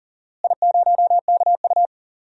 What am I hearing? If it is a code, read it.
Morse code, heard as a keyed tone, about 40 wpm, 700 Hz. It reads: I0KU
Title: DXCC-100 Call Signs - 4 Characters - Repeated 3x - 40wpm